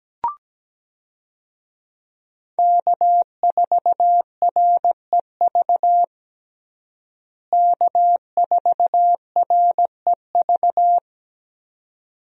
Morse code, heard as a keyed tone, about 17 wpm, 700 Hz. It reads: K4REV K4REV
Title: USA Call Signs - 5 Characters - Repeated 2x - 17wpm